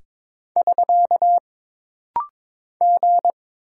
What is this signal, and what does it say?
Morse code, heard as a keyed tone, about 22 wpm, 700 Hz. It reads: <SK> G
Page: This rapid-fire timing is geared to push your ICR (Instant Character Recognition) skills.